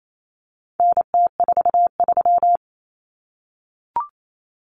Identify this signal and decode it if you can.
Morse code, heard as a keyed tone, about 28 wpm, 700 Hz. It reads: NT43
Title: Common Cut Numbers - 4-Digit 28wpm